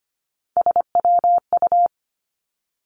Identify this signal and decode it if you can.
Morse code, heard as a keyed tone, about 25 wpm, 700 Hz. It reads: SWU
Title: Three Challenging Letter-Number Combinations 25wpm